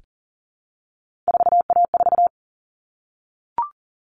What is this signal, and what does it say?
Morse code, heard as a keyed tone, about 40 wpm, 700 Hz. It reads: 4A4
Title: Common Cut Numbers - 3-Digit 40wpm